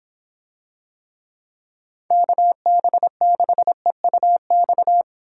Morse code, heard as a keyed tone, about 26 wpm, 700 Hz. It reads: KB6EUX